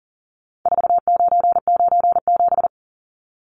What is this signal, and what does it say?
Morse code, heard as a keyed tone, about 40 wpm, 700 Hz. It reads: 4997
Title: Numbers - 4-Digit 40wpm